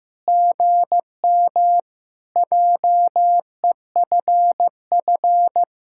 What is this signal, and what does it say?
Morse code, heard as a keyed tone, about 15 wpm, 700 Hz. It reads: GM JEFF
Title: SOTA QSO Practice 15wpm